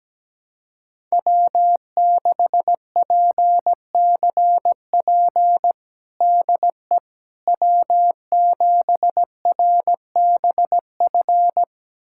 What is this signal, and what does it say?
Morse code, heard as a keyed tone, about 17 wpm, 700 Hz. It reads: W6PCP DE W7RBF
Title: USA Callsign DE Callsign - 5 Characters - 17wpm